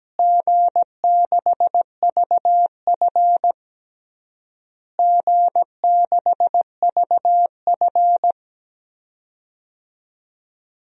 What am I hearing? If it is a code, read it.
Morse code, heard as a keyed tone, about 17 wpm, 700 Hz. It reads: G6VF G6VF
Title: DXCC-100 Call Signs - 4 Characters - Repeated 3x - 17wpm